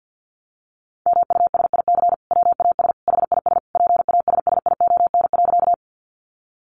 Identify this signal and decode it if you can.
Morse code, heard as a keyed tone, about 50 wpm, 700 Hz. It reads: MUSIC WAS HIS PASSION.
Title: Sentences from Top 1300 Words 50wpm